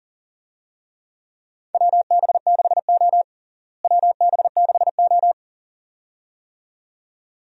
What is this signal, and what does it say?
Morse code, heard as a keyed tone, about 37 wpm, 700 Hz. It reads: WB6O WB6O
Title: USA Call Signs - 4 Characters - Repeated 2x - 40wpm